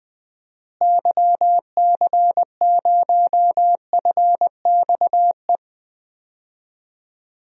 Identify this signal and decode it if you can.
Morse code, heard as a keyed tone, about 20 wpm, 700 Hz. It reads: YC0FXE
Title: DXCC-100 Call Signs - 6 Characters - Repeated 2x - 20wpm